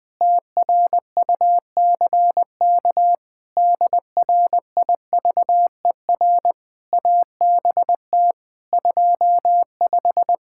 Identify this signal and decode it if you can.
Morse code, heard as a keyed tone, about 20 wpm, 700 Hz. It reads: TRUCK DRIVER ABT 25
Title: Practice QSO - Exchange 3 - More Info - 20wpm